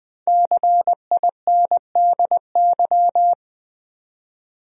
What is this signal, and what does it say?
Morse code, heard as a keyed tone, about 21 wpm, 700 Hz. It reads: CINDY
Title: USA Common Names - 20wpm